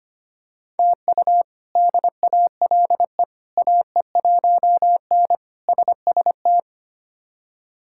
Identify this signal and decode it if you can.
Morse code, heard as a keyed tone, about 25 wpm, 700 Hz. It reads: TU DALE AE1N SST
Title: K1USN SST QSO Practice - 25wpm